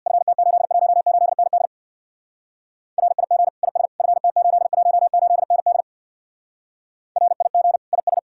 Morse code, heard as a keyed tone, about 66 wpm, 700 Hz. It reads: FT818ND RIG IS FT818ND RIG IS